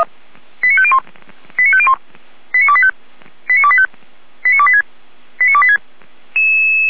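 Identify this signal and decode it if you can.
Five-tone selective call (ZVEI), heard as quick runs of five tones, tones of about 70 ms. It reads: 89501 89501 89285 89285 89286 89286
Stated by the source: Received by SDR, ZVEI five-tone selective call.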